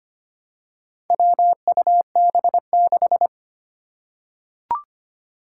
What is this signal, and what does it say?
Morse code, heard as a keyed tone, about 25 wpm, 700 Hz. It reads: WUB6
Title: Four Challenging Letter-Number Combinations 25wpm